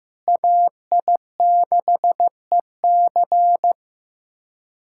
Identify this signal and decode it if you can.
Morse code, heard as a keyed tone, about 15 wpm, 700 Hz. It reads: AI6EC
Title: USA Callsign DE Callsign - 5 Characters - 15wpm